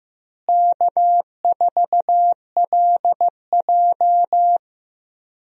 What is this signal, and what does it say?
Morse code, heard as a keyed tone, about 15 wpm, 700 Hz. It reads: K4LJ
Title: USA Call Signs - 4 Characters - Repeated 3x - 15wpm